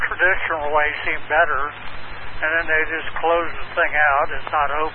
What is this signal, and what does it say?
Emergency Alert System (SAME) tones, heard as harsh buzzing bursts, with speech. It is heard from 0.9 to 4.1 s.